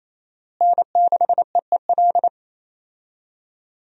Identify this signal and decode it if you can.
Morse code, heard as a keyed tone, about 28 wpm, 700 Hz. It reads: N6EEL
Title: USA Call Signs - 5 Characters - Repeated 2x - 28wpm